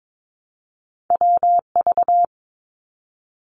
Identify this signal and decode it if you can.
Morse code, heard as a keyed tone, about 22 wpm, 700 Hz. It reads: WV
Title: US States 22wpm